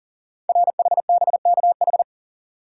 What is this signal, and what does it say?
Morse code, heard as a keyed tone, about 40 wpm, 700 Hz. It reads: RHBKH
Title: Code Groups - 5 Letters - 40wpm